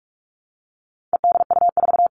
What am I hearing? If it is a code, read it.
Morse code, heard as a keyed tone, about 44 wpm, 700 Hz. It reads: EDU4
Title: All Cut Numbers - 4-Digit 45wpm